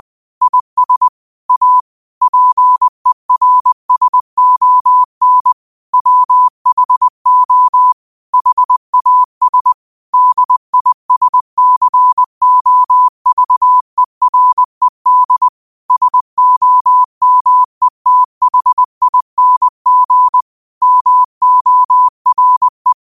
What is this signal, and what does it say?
Morse code, heard as a keyed tone, about 20 wpm, 1000 Hz. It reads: IS A PERSON WHO HAS DISCOVERED SOMETHING MORE
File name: qotdfiles / 20wpm